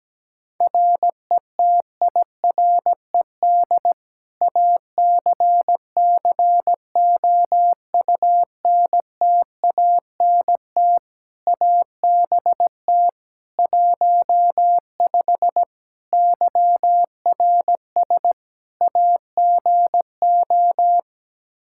Morse code, heard as a keyed tone, about 17 wpm, 700 Hz. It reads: RETIRED ACCOUNTANT ABT 15 YRS AGO
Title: Practice QSO - Exchange 3 - More Info - 17wpm